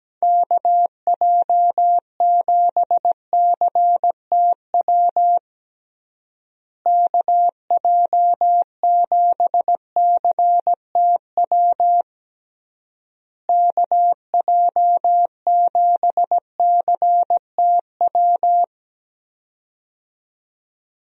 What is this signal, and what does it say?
Morse code, heard as a keyed tone, about 17 wpm, 700 Hz. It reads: KJ7CTW KJ7CTW KJ7CTW
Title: USA Call Signs - 6 Characters - Repeated 3x - 17wpm